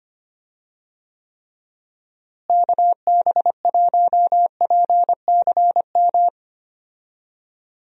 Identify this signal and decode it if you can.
Morse code, heard as a keyed tone, about 25 wpm, 700 Hz. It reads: KB1PCM
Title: USA Call Signs - 6 Characters - Repeated 3x - 25wpm